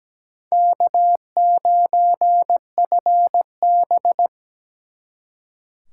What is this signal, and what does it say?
Morse code, heard as a keyed tone, about 20 wpm, 700 Hz. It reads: K9FB
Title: Speed Racing - USA 1x2 Call Signs 17wpm